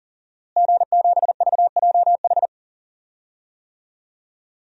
Morse code, heard as a keyed tone, about 40 wpm, 700 Hz. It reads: G7VJH
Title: DXCC-100 Call Signs - 5 Characters - Repeated 3x - 40wpm